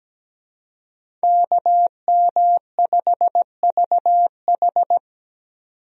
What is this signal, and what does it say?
Morse code, heard as a keyed tone, about 17 wpm, 700 Hz. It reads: KM5VH
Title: USA Call Signs - 5 Characters - 17wpm